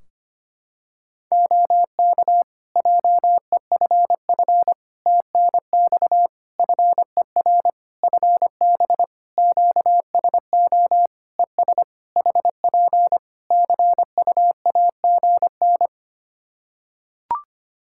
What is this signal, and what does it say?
Morse code, heard as a keyed tone, about 25 wpm, 700 Hz. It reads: OK JEFF TNX FER FB QSO ES HP CUAGN
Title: Practice QSO - Exchange 4 - Ending - 25wpm